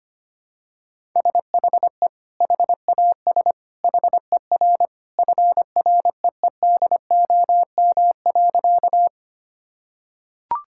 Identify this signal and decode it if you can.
Morse code, heard as a keyed tone, about 25 wpm, 700 Hz. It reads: SHE HAS HER FREEDOM.
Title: Sentences from Top 1000 Words - Review Part 5 - 25wpm